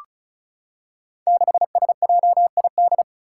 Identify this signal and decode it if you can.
Morse code, heard as a keyed tone, about 36 wpm, 700 Hz. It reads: 6SJID
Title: Five Challenging Letter-Number Combinations 35wpm